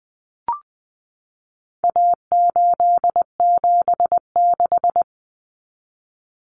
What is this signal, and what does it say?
Morse code, heard as a keyed tone, about 20 wpm, 700 Hz. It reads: A876
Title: Common Cut Numbers - 4-Digit 20wpm